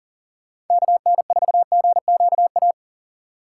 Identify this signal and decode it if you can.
Morse code, heard as a keyed tone, about 50 wpm, 700 Hz. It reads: KN4GQA
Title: Speed Racing - USA 2x3 Call Signs 40wpm